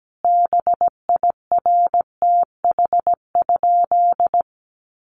Morse code, heard as a keyed tone, about 17 wpm, 700 Hz. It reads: BIRTH?
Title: Sentences from Top 800 Words 17wpm